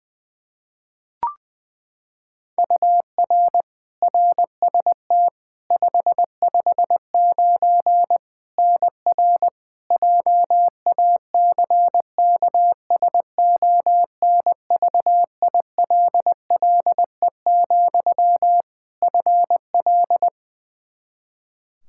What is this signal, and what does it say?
Morse code, heard as a keyed tone, about 20 wpm, 700 Hz. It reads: UR RST 559 NR JACKSONVILLE, FL
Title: Practice QSO - Exchange 1 - Beginning - 20wpm